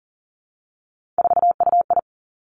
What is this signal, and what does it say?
Morse code, heard as a keyed tone, about 40 wpm, 700 Hz. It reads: 4UI
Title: Three Challenging Letter-Number Combinations 40wpm